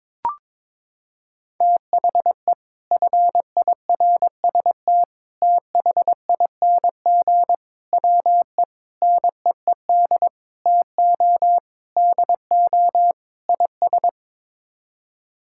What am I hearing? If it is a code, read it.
Morse code, heard as a keyed tone, about 22 wpm, 700 Hz. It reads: THE FIRST THING WE NEED TO DO IS
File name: practice / 22wpm / Sets of 8 Words - 22wpm